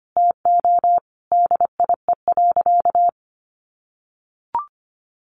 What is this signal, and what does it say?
Morse code, heard as a keyed tone, about 25 wpm, 700 Hz. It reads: TO DIE.
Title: Sentences from Top 1300 Words 25wpm